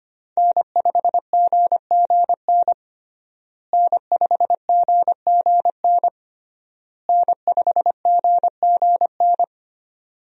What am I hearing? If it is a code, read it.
Morse code, heard as a keyed tone, about 25 wpm, 700 Hz. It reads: N5GGN N5GGN N5GGN
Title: USA Call Signs - 5 Characters - Repeated 3x - 25wpm